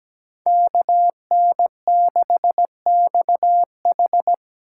Morse code, heard as a keyed tone, about 20 wpm, 700 Hz. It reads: KN6XH
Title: Speed Racing - USA Call Signs 17wpm